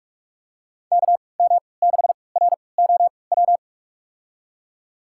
Morse code, heard as a keyed tone, about 46 wpm, spaced as for 26 wpm, 700 Hz. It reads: KM6ROW
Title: USA Call Signs - 3x Character Spacing - 45wpm